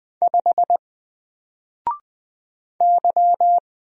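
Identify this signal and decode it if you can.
Morse code, heard as a keyed tone, about 20 wpm, 700 Hz. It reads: 5 Y
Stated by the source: Single letter and number Morse drill at 20 wpm, from N4LCD.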